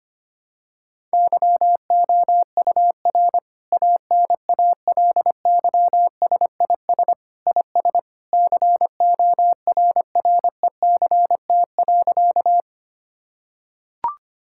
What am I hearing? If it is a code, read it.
Morse code, heard as a keyed tone, about 25 wpm, 700 Hz. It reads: YOUR ANALYSIS IS CORRECT.
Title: Sentences from Top 1000 Words - Review Part 5 - 25wpm